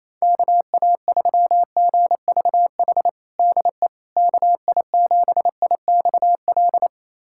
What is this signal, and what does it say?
Morse code, heard as a keyed tone, about 28 wpm, 700 Hz. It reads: KA3GVH DE KI7IXL
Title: USA Callsign DE Callsign - 6 Characters - 28wpm